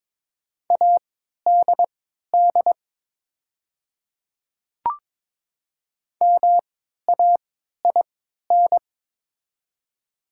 Morse code, heard as a keyed tone, about 22 wpm, spaced as for 13 wpm, 700 Hz. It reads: ADD MAIN
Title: 3x Character Spacing - Top 300 Words - 22wpm